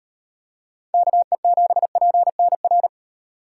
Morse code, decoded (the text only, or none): KE7PNR